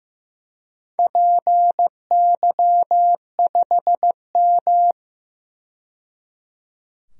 Morse code, heard as a keyed tone, about 15 wpm, 700 Hz. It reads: PY5M
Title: DXCC-100 Call Signs - 4 Characters - 15wpm